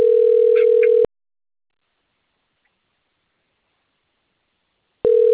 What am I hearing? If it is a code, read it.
Ringback tone (North America), 440 + 480 Hz.